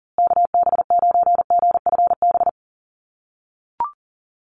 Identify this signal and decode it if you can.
Morse code, heard as a keyed tone, about 38 wpm, 700 Hz. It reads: KB9GFB
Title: USA Call Signs Repeated 2x 40wpm